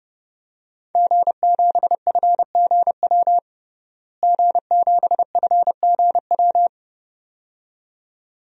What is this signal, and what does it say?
Morse code, heard as a keyed tone, about 30 wpm, 700 Hz. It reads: G7FGW G7FGW
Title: DXCC-100 Call Signs - 5 Characters - Repeated 2x - 30wpm